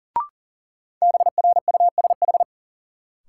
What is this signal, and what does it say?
Morse code, heard as a keyed tone, about 40 wpm, 700 Hz. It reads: BRUSH
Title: Sentences from Top 1100 Words - New Word Drill Alt - 40wpm